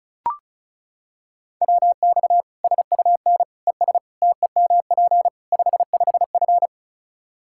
Morse code, heard as a keyed tone, about 34 wpm, 700 Hz. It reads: WX SUN ES TEMP 55F
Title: K1USN SST QSO Practice 35wpm